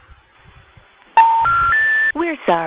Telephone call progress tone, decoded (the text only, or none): special information tone